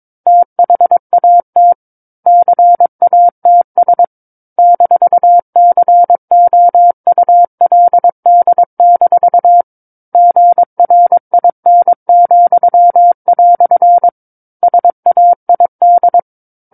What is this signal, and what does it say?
Morse code, heard as a keyed tone, about 22 wpm, 700 Hz. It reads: THAT CATS -COULD- GRIN," SAID
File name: 06-Chapter-06 22 WPM